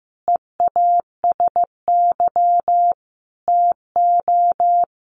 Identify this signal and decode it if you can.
Morse code, heard as a keyed tone, about 15 wpm, 700 Hz. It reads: EASY TO
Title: Sentences from Top 1500 Words 15wpm